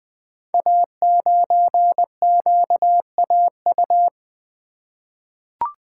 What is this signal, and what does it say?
Morse code, heard as a keyed tone, about 20 wpm, 700 Hz. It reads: A9QAU
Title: Code Groups - 5 Letter:Numbers - 20wpm